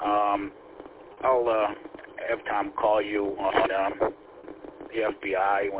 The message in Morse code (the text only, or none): none